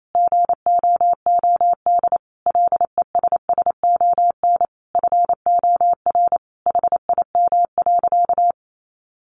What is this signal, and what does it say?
Morse code, heard as a keyed tone, about 28 wpm, 700 Hz. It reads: GOOD LESSON FOR HIM.
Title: Sentences from Top 1200 Words 28wpm